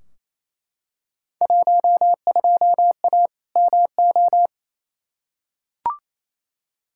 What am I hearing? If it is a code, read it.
Morse code, heard as a keyed tone, about 28 wpm, 700 Hz. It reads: 12A MO
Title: Field Day Exchanges Encore - 28wpm